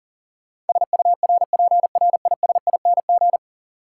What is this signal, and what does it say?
Morse code, heard as a keyed tone, about 40 wpm, 700 Hz. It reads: SURPRISING
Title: Sets of 5 Words - New Word Drill - 40wpm